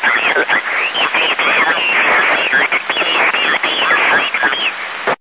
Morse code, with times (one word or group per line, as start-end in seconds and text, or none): none